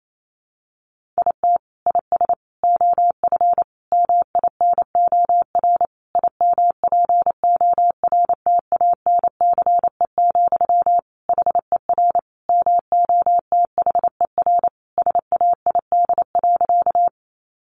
Morse code, heard as a keyed tone, about 28 wpm, 700 Hz. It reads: IT IS OF MINOR IMPORTANCE, HER MOTHER SAID.
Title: Sentences from Top 1500 Words 28wpm